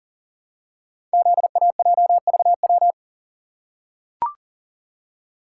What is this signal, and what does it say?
Morse code, heard as a keyed tone, about 39 wpm, 700 Hz. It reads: ZAJVW
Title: Code Groups - 5 Letters - 40wpm